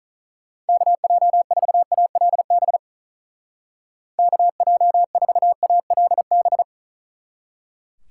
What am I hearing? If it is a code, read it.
Morse code, heard as a keyed tone, about 45 wpm, 700 Hz. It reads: KJ4ALB KJ4ALB